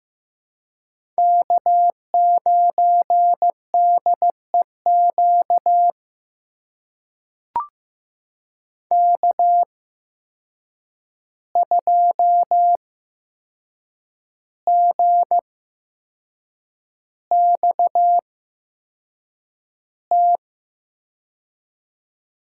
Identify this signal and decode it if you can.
Morse code, heard as a keyed tone, about 15 wpm, spaced as for 6 wpm, 700 Hz. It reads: K9DEQ K2GXT
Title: USA Call Signs - 5 Characters - 8x Character Spacing - 15wpm